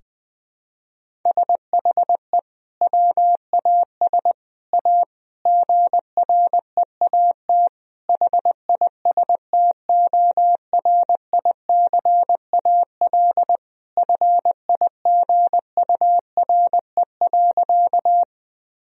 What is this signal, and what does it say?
Morse code, heard as a keyed tone, about 21 wpm, 700 Hz. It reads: SHE WAS A GREAT HISTORICAL FIGURE.
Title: Sentences from Top 1000 Words - Review Part 4 - 20wpm